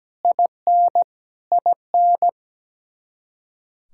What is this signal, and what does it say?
Morse code, heard as a keyed tone, about 17 wpm, 700 Hz. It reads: IN IN
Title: Practice QSO - Exchange 1 - Beginning - 17wpm